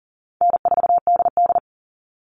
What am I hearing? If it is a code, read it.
Morse code, heard as a keyed tone, about 40 wpm, 700 Hz. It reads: N4DD
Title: Call Signs Repeated 2x - 40wpm